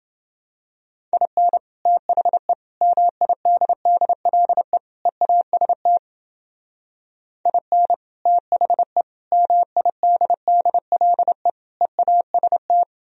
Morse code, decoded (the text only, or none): IN THE MIDDLE EAST IN THE MIDDLE EAST